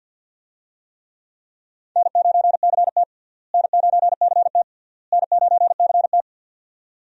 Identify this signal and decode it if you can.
Morse code, heard as a keyed tone, about 52 wpm, 700 Hz. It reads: N9CT N9CT N9CT
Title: USA Call Signs - 4 Characters - Repeated 3x - 50wpm